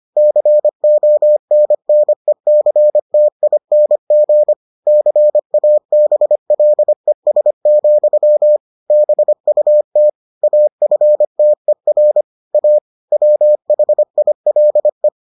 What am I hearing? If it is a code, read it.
Morse code, heard as a keyed tone, about 25 wpm, 600 Hz. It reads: CONNECTING CABLES, BUT AFTER A WHILE